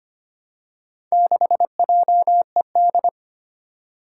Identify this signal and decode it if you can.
Morse code, heard as a keyed tone, about 25 wpm, 700 Hz. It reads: 6JED
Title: Four Challenging Letter-Number Combinations 25wpm